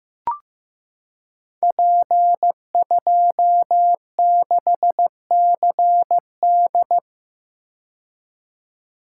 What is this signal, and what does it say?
Morse code, heard as a keyed tone, about 15 wpm, 700 Hz. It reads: P26CD
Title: Code Groups - 5 Letter:Numbers - 15wpm